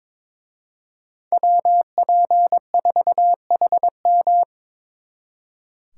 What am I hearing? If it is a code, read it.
Morse code, heard as a keyed tone, about 26 wpm, 700 Hz. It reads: WP4HM
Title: Speed Racing - USA 2x2 Call Signs 22wpm